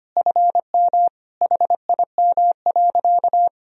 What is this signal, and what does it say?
Morse code, heard as a keyed tone, about 25 wpm, 700 Hz. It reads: FM HIM.
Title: Sentences from Top 400 Words and Abbreviations 25wpm